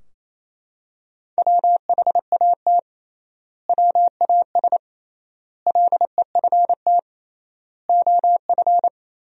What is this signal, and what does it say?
Morse code, heard as a keyed tone, about 28 wpm, 700 Hz. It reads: WHAT WAS LEFT OF
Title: 3x Word-Spacing - Sentences from Top 300 Words - 28wpm